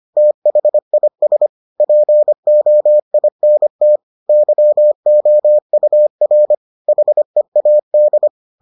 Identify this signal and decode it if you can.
Morse code, heard as a keyed tone, about 25 wpm, 600 Hz. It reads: THIS POINT YOUR HEAD